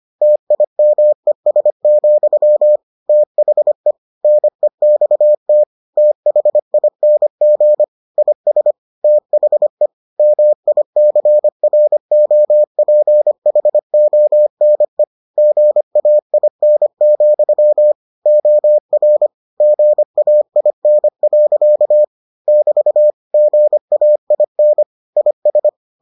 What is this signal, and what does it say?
Morse code, heard as a keyed tone, about 25 wpm, 600 Hz. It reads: TIMES, THE NEXT THING IS THE MICROPHONE GAIN, OR GAIN. <BT> GAIN IS